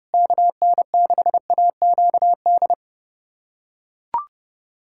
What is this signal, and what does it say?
Morse code, heard as a keyed tone, about 30 wpm, 700 Hz. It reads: KN6AQD
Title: USA Callsign DE Callsign - 6 Characters - 30wpm